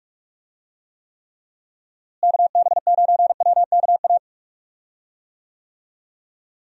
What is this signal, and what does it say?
Morse code, heard as a keyed tone, about 42 wpm, 700 Hz. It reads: KB9WKA
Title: USA Call Signs - 6 Characters - Repeated 3x - 45wpm